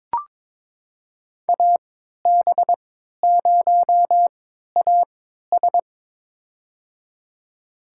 Morse code, heard as a keyed tone, about 22 wpm, spaced as for 13 wpm, 700 Hz. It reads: AB0AS
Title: USA Call Signs - 3x Character Spacing - 22wpm